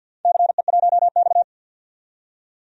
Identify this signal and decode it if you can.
Morse code, heard as a keyed tone, about 52 wpm, 700 Hz. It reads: CE1X
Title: DXCC-100 Call Signs - 4 Characters - 50wpm